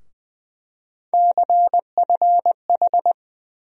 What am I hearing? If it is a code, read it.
Morse code, heard as a keyed tone, about 20 wpm, 700 Hz. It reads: CFH
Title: USA Call Sign Postfixes - 3 Letters 20wpm